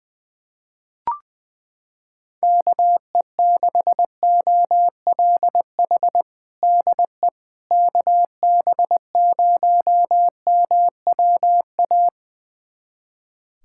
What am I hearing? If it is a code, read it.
Morse code, heard as a keyed tone, about 20 wpm, 700 Hz. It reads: KE6OLH DE KB0MWA
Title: USA Callsign DE Callsign - 6 Characters - 20wpm